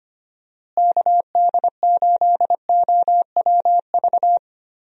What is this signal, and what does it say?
Morse code, heard as a keyed tone, about 25 wpm, 700 Hz. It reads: KD8OWV
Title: USA Call Signs - 6 Characters - Repeated 2x - 25wpm